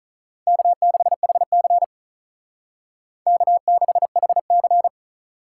Morse code, heard as a keyed tone, about 45 wpm, 700 Hz. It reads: K6HC K6HC